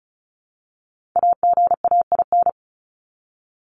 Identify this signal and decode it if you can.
Morse code, heard as a keyed tone, about 34 wpm, 700 Hz. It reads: AGAIN
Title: Top 200 Words 35wpm